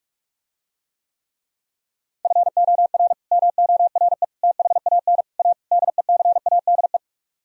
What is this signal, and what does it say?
Morse code, heard as a keyed tone, about 45 wpm, 700 Hz. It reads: FOR MORE THAN A DECADE